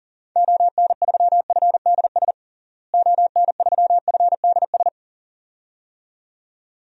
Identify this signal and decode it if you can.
Morse code, heard as a keyed tone, about 37 wpm, 700 Hz. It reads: ON3FDS ON3FDS
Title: DXCC-100 Call Signs - 6 Characters - Repeated 2x - 40wpm